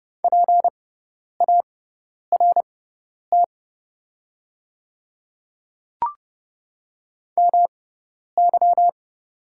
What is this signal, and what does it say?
Morse code, heard as a keyed tone, about 30 wpm, spaced as for 10 wpm, 700 Hz. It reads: PART MY
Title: 6x Character Spacing - Top 200 Words - 30wpm